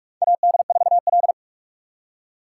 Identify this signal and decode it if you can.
Morse code, heard as a keyed tone, about 42 wpm, 700 Hz. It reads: AD4L
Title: USA Call Signs - 4 Characters - 3x Character Spacing - 45wpm